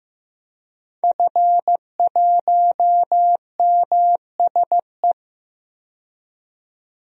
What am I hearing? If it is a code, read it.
Morse code, heard as a keyed tone, about 15 wpm, 700 Hz. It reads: F1MSE
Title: DXCC-100 Call Signs - 5 Characters - 15wpm